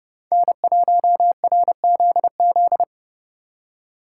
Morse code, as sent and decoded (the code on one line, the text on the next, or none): -. .---- .-. --.. --..
N1RZZ